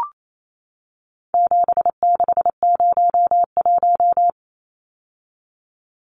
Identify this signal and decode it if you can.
Morse code, heard as a keyed tone, about 28 wpm, 700 Hz. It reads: 7601
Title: Numbers - 4-Digit 28wpm